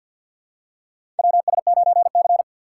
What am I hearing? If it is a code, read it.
Morse code, heard as a keyed tone, about 50 wpm, 700 Hz. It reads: WS9C
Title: USA Call Signs - 2x Character Spacing - 50wpm